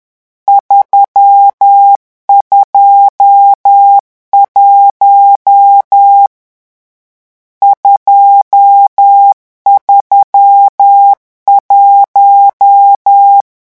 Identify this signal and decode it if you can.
Morse code, heard as a keyed tone, about 10 wpm, 800 Hz. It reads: 321 231
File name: morse mp3 / lecon10-vitesse 10